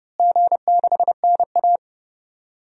Morse code, heard as a keyed tone, about 30 wpm, 700 Hz. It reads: G6NA
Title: DXCC-100 Call Signs - 4 Characters - Repeated 2x - 30wpm